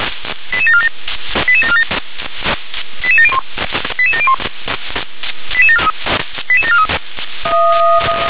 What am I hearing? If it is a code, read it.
Five-tone selective call (ZVEI), heard as quick runs of five tones, tones of about 70 ms. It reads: 88747 88747 88701 88701 88743
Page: SDR/ZVEI